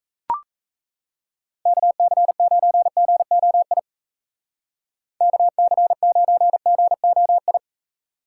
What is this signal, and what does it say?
Morse code, heard as a keyed tone, about 35 wpm, 700 Hz. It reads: KC9GOI KC9GOI